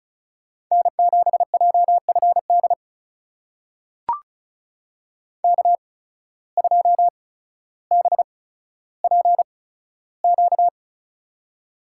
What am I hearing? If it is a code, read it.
Morse code, heard as a keyed tone, about 36 wpm, spaced as for 15 wpm, 700 Hz. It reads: N7JFD K2BPQ
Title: USA Call Signs - 5 Characters - 8x Character Spacing - 35wpm